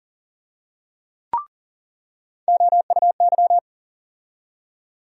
Morse code, heard as a keyed tone, about 40 wpm, 700 Hz. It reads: OUY
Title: USA Call Sign Postfixes - 3 Letters 40wpm